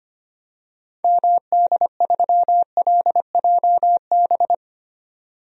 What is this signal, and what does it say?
Morse code, heard as a keyed tone, about 25 wpm, 700 Hz. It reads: MD3LJB
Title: DXCC-100 Call Signs - 6 Characters - 25wpm